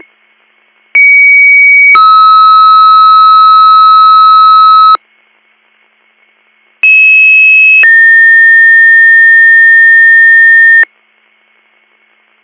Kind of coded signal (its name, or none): two-tone page